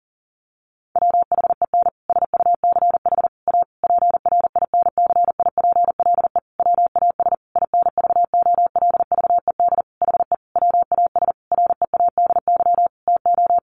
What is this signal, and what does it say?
Morse code, heard as a keyed tone, about 40 wpm, 700 Hz. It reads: WHEN SUCH A PRINCIPLE WAS INVOLVED HE WAS READY TO
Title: Sentences from Top 1400 Words 40wpm